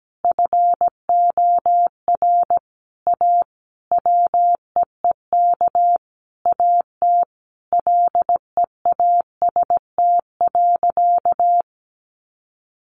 Morse code, heard as a keyed tone, about 17 wpm, 700 Hz. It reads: FOR A WEEK AT LEAST.
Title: Sentences from Top 900 Words 17wpm